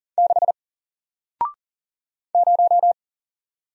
Morse code, heard as a keyed tone, about 40 wpm, 700 Hz. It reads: 6 0
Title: Single Letter-Number 40wpm